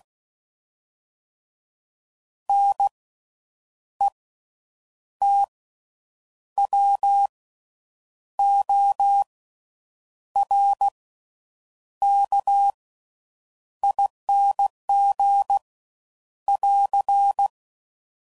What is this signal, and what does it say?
Morse code, heard as a keyed tone, about 16 wpm, spaced as for 7 wpm, 800 Hz. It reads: NETWORKING <AR>
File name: Suffixes / Special INGs / Slow With Faster Ings